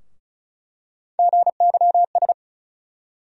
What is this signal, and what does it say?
Morse code, heard as a keyed tone, about 34 wpm, 700 Hz. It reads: GYS
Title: USA Call Sign Postfixes - 3 Letters 35wpm